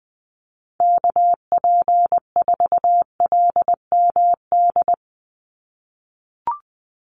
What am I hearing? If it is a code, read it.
Morse code, heard as a keyed tone, about 20 wpm, 700 Hz. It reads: KP4LMD
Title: USA Call Signs 20wpm